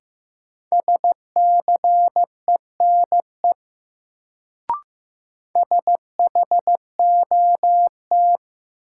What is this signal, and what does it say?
Morse code, heard as a keyed tone, about 15 wpm, 700 Hz. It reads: SCENE SHOT
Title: Sentences from Top 1100 Words - New Word Drill - 15wpm